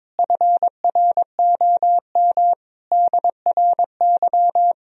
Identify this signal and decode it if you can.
Morse code, heard as a keyed tone, about 22 wpm, 700 Hz. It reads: FROM DRY
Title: Sentences from Top 1000 Words - Review Part 3 - 22wpm